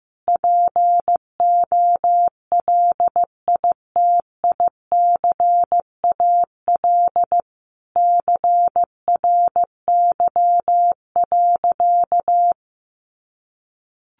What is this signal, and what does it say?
Morse code, heard as a keyed tone, about 15 wpm, 700 Hz. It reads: POLITICAL CRY.
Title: Sentences from Top 1400 Words 15wpm